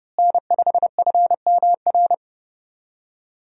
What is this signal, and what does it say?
Morse code, heard as a keyed tone, about 30 wpm, 700 Hz. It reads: N5FMR
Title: USA Call Signs - 5 Characters - Repeated 3x - 30wpm